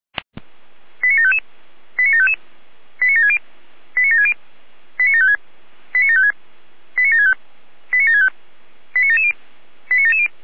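Five-tone selective call (ZVEI), heard as quick runs of five tones, tones of about 70 ms. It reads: 89755 89755 89760 89760 89756 89756 89765 89765 89770 89770
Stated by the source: Five-tone ZVEI selective call 89701, received by SDR.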